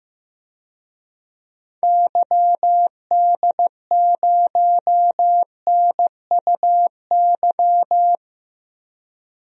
Morse code, heard as a keyed tone, about 15 wpm, 700 Hz. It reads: YD0NUY
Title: DXCC-100 Call Signs - 6 Characters - 15wpm